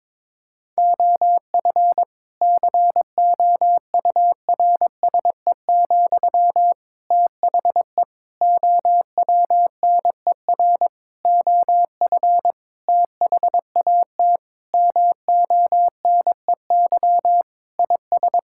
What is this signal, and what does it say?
Morse code, heard as a keyed tone, about 22 wpm, 700 Hz. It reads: OF COURSE, THE OWNER OF THAT MONEY IS
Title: Sentences from Top 1600 Words 22wpm